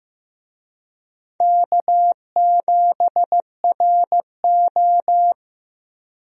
Morse code, heard as a keyed tone, about 15 wpm, 700 Hz. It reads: K7RO
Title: USA Call Signs - 4 Characters - Repeated 2x - 15wpm